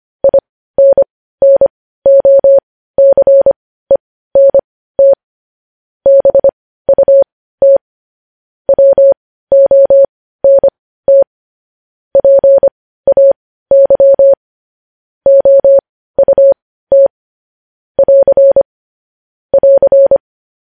Morse code, heard as a keyed tone, about 25 wpm, spaced as for 15 wpm, 550 Hz. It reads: INNOCENT BUT WONT PAY OUT <AR> <AR>
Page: news headlines in morse code at 15 wpm